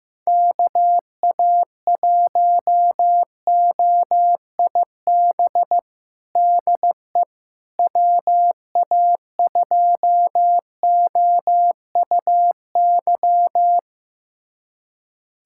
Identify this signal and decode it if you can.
Morse code, heard as a keyed tone, about 15 wpm, 700 Hz. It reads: KA1OIB DE WA2OUY
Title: USA Callsign DE Callsign 15wpm